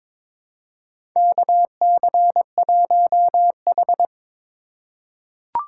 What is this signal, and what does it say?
Morse code, heard as a keyed tone, about 22 wpm, 700 Hz. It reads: KC1H
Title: USA Call Signs - 4 Characters - 4x Character Spacing - 22wpm